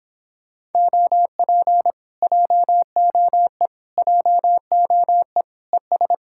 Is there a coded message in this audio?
Yes: Morse code from 0.8 to 6.1 s, heard as a keyed tone, about 26 wpm, 700 Hz. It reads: OP JOE JOE ES